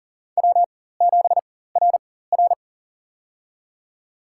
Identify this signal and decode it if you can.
Morse code, heard as a keyed tone, about 40 wpm, spaced as for 19 wpm, 700 Hz. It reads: W7RR
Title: USA Call Signs - 4 Characters - 4x Character Spacing - 40wpm